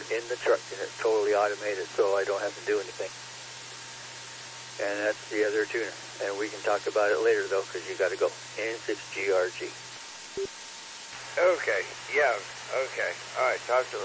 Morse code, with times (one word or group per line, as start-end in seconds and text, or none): none